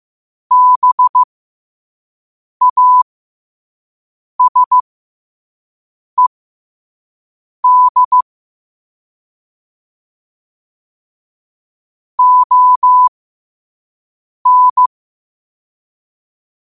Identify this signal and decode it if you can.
Morse code, heard as a keyed tone, about 15 wpm, spaced as for 5 wpm, 1000 Hz. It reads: BASED ON
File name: qotdfiles / 5wpm